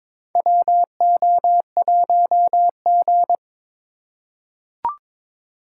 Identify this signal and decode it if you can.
Morse code, heard as a keyed tone, about 21 wpm, 700 Hz. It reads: WO1G
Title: USA Call Signs - 4 Characters - 4x Character Spacing - 22wpm